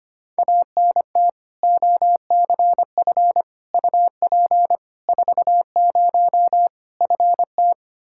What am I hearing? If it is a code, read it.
Morse code, heard as a keyed tone, about 25 wpm, 700 Hz. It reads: ANT OCF UP 40 FT
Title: K1USN SST QSO Practice 25wpm